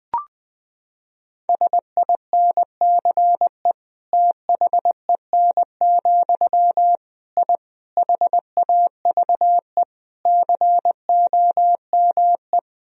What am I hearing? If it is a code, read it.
Morse code, heard as a keyed tone, about 20 wpm, 700 Hz. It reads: SINCE THEN, I HAVE COME
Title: Sentences from Top 1000 Words - Review Part 1 - 20wpm